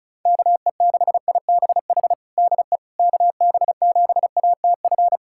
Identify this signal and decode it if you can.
Morse code, heard as a keyed tone, about 36 wpm, 700 Hz. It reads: KE6IBH DE KB7ATF
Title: USA Callsign DE Callsign - 6 Characters - 35wpm